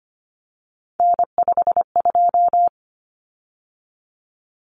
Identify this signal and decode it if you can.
Morse code, heard as a keyed tone, about 25 wpm, 700 Hz. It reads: N52
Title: Common Cut Numbers - 3-Digit 25wpm